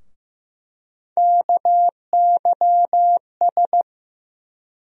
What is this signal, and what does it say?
Morse code, heard as a keyed tone, about 15 wpm, 700 Hz. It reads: KYS
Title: USA Call Sign Postfixes - 3 Letters 15wpm